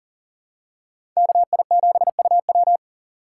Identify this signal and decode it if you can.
Morse code, heard as a keyed tone, about 40 wpm, 700 Hz. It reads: KI7UW
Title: USA Call Signs - 5 Characters - 6x Character Spacing - 40wpm